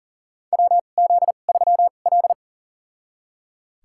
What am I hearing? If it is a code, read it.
Morse code, heard as a keyed tone, about 40 wpm, spaced as for 29 wpm, 700 Hz. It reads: WZ3L
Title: USA Call Signs - 4 Characters - 2x Character Spacing - 40wpm